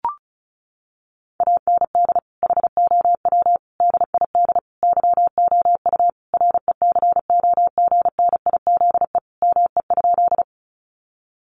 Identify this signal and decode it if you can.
Morse code, heard as a keyed tone, about 34 wpm, 700 Hz. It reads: AND HOW DID YOU RECOGNIZE ME?
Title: Sentences from Top 1400 Words 35wpm